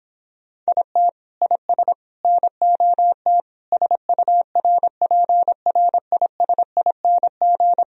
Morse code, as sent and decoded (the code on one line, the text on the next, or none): .. - / .. ... / -. --- - / ... ..- .-. .--. .-. .. ... .. -. --.
IT IS NOT SURPRISING